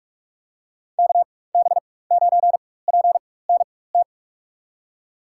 Morse code, heard as a keyed tone, about 46 wpm, spaced as for 21 wpm, 700 Hz. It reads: KB9PNT
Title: USA Call Signs - 6 Characters - 4x Character Spacing - 45wpm